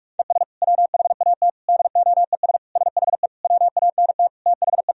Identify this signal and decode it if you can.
Morse code, heard as a keyed tone, about 46 wpm, 700 Hz. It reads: ES WHAT DOES SHE WANT THE
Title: Sentences from Top 400 Words and Abbreviations 45wpm